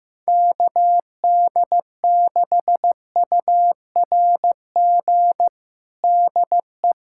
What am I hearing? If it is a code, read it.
Morse code, heard as a keyed tone, about 15 wpm, 700 Hz. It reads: KD6URG DE
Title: USA Callsign DE Callsign 15wpm